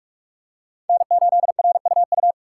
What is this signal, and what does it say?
Morse code, heard as a keyed tone, about 41 wpm, 700 Hz. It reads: N8RUU
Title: USA Call Signs - 5 Characters - 2x Character Spacing - 45wpm